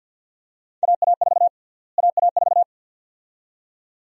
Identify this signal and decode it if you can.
Morse code, heard as a keyed tone, about 51 wpm, 700 Hz. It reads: AA4 AA4
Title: USA Call Sign Prefixes - Repeated 2x - 50wpm